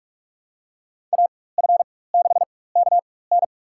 Morse code, heard as a keyed tone, about 43 wpm, spaced as for 21 wpm, 700 Hz. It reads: AF6KN
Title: USA Call Signs - 5 Characters - 4x Character Spacing - 45wpm